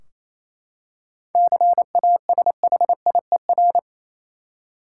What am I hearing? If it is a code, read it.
Morse code, heard as a keyed tone, about 28 wpm, 700 Hz. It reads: CASHIER
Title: Practice QSO - Exchange 3 - More Info - 28wpm